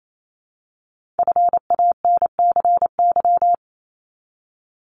Morse code, heard as a keyed tone, about 28 wpm, 700 Hz. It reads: FANCY
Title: Multi-letter Phonemes Repeated 3x 28wpm